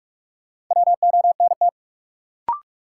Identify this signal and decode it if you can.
Morse code, heard as a keyed tone, about 41 wpm, 700 Hz. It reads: WONT
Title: Sentences from Top 1000 Words - Review Part 6 - New Word Drill - 45wpm